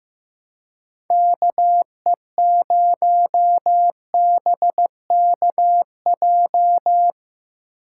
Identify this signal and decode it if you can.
Morse code, heard as a keyed tone, about 17 wpm, 700 Hz. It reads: KE0BKJ
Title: Speed Racing - USA 2x3 Call Signs 15wpm